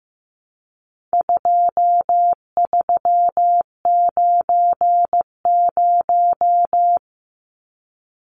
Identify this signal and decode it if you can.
Morse code, heard as a keyed tone, about 15 wpm, 700 Hz. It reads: 2390
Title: Numbers - 4-Digit 15wpm